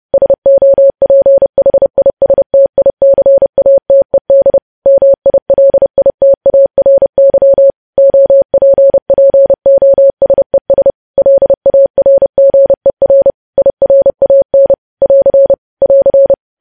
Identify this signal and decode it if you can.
Morse code, heard as a keyed tone, about 30 wpm, 550 Hz. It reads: SOPHISTICATED MILITARY OPPOSES LARGER IRAN <AR> <AR>
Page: news headlines in morse code at 30 wpm